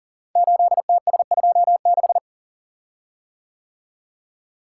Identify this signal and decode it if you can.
Morse code, heard as a keyed tone, about 41 wpm, 700 Hz. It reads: 8TS26
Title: Code Groups - 5 Letter:Numbers - 40wpm